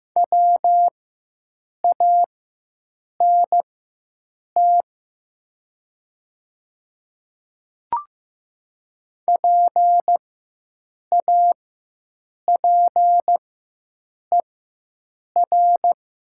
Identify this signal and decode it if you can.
Morse code, heard as a keyed tone, about 15 wpm, spaced as for 7 wpm, 700 Hz. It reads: WANT PAPER
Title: 4x Character Spacing - Top 300 Words - 15wpm